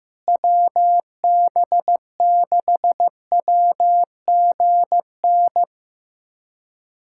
Morse code, heard as a keyed tone, about 15 wpm, 700 Hz. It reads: WB6WGN
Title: USA Callsign DE Callsign - 6 Characters - 15wpm